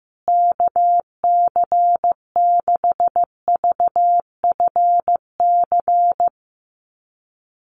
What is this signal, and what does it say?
Morse code, heard as a keyed tone, about 14 wpm, 700 Hz. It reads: KC6VFC
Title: USA Call Signs Repeated 2x 15wpm